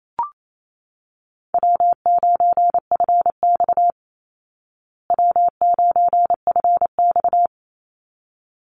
Morse code, heard as a keyed tone, about 28 wpm, 690 Hz. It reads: W9FX W9FX
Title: Call Signs Repeated 3x - 28wpm